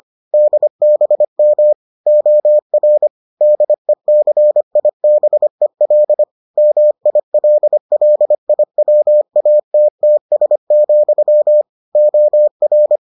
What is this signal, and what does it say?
Morse code, heard as a keyed tone, about 25 wpm, 600 Hz. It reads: DBM OR DECIBEL MILLIWATTS, OR